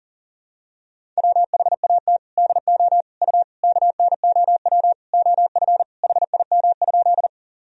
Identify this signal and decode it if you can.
Morse code, heard as a keyed tone, about 40 wpm, 700 Hz. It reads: WHAT DO U KNOW OF HIM?
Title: Sentences from Top 100 Words and Abbreviations 40wpm